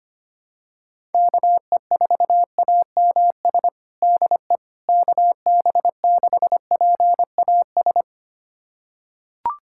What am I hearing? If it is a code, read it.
Morse code, heard as a keyed tone, about 25 wpm, 700 Hz. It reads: KE4AMS DE KB6PAS
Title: USA Callsign DE Callsign 25wpm